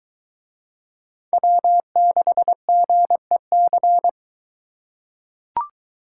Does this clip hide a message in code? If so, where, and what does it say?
Yes: Morse code from 1.3 to 4.1 s, heard as a keyed tone, about 22 wpm, 700 Hz. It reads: W6GEC